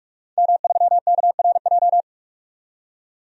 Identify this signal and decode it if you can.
Morse code, heard as a keyed tone, about 46 wpm, 700 Hz. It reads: M3KRJ
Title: DXCC-100 Call Signs - 5 Characters - 45wpm